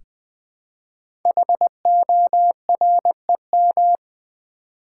Morse code, heard as a keyed tone, about 20 wpm, 700 Hz. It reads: HOREM
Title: Code Groups - 5 Letters - 20wpm